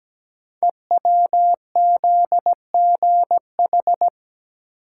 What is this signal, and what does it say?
Morse code, heard as a keyed tone, about 17 wpm, 700 Hz. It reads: EWZGH
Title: Code Groups - 5 Letters - 17wpm